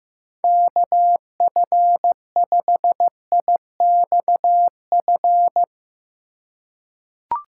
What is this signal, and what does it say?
Morse code, heard as a keyed tone, about 15 wpm, 700 Hz. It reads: KF5IXF
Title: USA Call Signs - 6 Characters - 2x Character Spacing - 15wpm